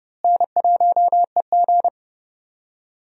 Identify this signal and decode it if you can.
Morse code, heard as a keyed tone, about 32 wpm, 700 Hz. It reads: N1EG
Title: USA Call Signs - 4 Characters - Repeated 3x - 30wpm